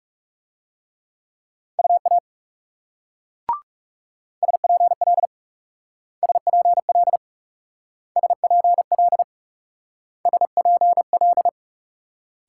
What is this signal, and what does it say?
Morse code, heard as a keyed tone, about 37 wpm, 700 Hz. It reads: UA SPL SPL SPL SPL
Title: Speed-Racing Multi-letter Phonemes - 30wpm